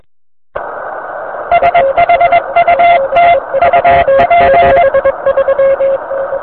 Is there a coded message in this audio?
Yes: Morse code from 1.5 to 4.8 s, heard as a keyed tone, about 21 wpm, 690 Hz, with noise. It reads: SHUT UP